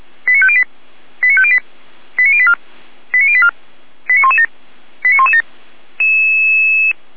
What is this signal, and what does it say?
Five-tone selective call (ZVEI), heard as quick runs of five tones, tones of about 70 ms. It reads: 89508 89508 89084 89084 89117 89117
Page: SDR/ZVEI